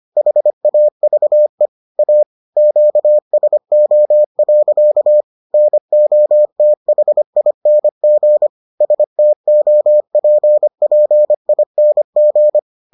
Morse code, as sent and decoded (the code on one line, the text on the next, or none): .... .- ...- . / .- / --.- ... --- .-.-.- / -. --- - .... .. -. --. / ... - --- .--. .--. .. -. --.
HAVE A QSO. NOTHING STOPPING